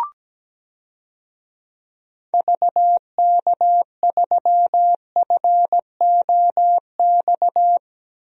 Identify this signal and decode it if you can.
Morse code, heard as a keyed tone, about 17 wpm, 700 Hz. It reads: VK3FOX
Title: DXCC-100 Call Signs - 6 Characters - Repeated 3x - 17wpm